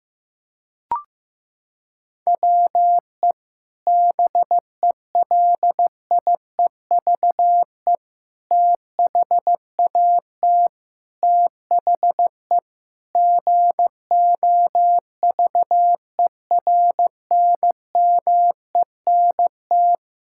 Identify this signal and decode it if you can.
Morse code, heard as a keyed tone, about 15 wpm, 700 Hz. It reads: WE BELIEVE THAT THE GOVERNMENT
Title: Sentences from Top 1000 Words - Review Part 6 - 15wpm